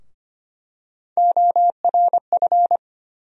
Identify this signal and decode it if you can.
Morse code, heard as a keyed tone, about 25 wpm, 700 Hz. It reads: ORF
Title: USA Call Sign Postfixes - 3 Letters 25wpm